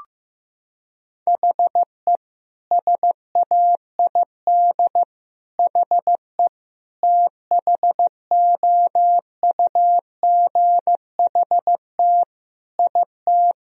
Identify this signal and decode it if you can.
Morse code, heard as a keyed tone, about 15 wpm, 700 Hz. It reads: HE SAID HE THOUGHT IT
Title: Sentences from Top 1000 Words - Review Part 3 - 15wpm